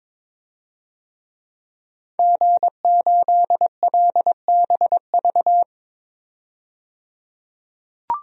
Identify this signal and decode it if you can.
Morse code, heard as a keyed tone, about 21 wpm, 700 Hz. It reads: G8LBV
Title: DXCC-100 Call Signs - 5 Characters - 22wpm